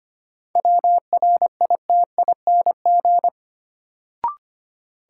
Morse code, heard as a keyed tone, about 25 wpm, 700 Hz. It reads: WRITING
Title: Sentences from Top 800 Words - New Word Drill - 25wpm